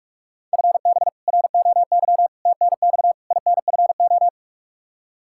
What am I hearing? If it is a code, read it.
Morse code, heard as a keyed tone, about 48 wpm, 700 Hz. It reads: FB ROY TNX INFO
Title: K1USN SST QSO Practice 45wpm